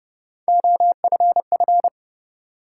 Morse code, heard as a keyed tone, about 30 wpm, 700 Hz. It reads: OFF
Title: Sentences from Top 200 Words - New Word Drill - 30wpm